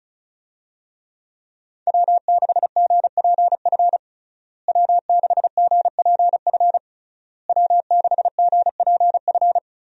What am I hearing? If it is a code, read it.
Morse code, heard as a keyed tone, about 36 wpm, 700 Hz. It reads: W6GPF W6GPF W6GPF
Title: USA Call Signs - 5 Characters - Repeated 3x - 35wpm